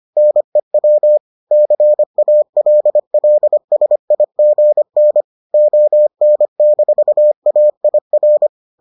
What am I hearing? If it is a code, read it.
Morse code, heard as a keyed tone, about 25 wpm, 600 Hz. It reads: NEW CALLSIGN ON-AIR